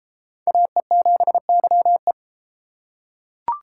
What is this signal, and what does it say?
Morse code, heard as a keyed tone, about 32 wpm, 700 Hz. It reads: AE7YE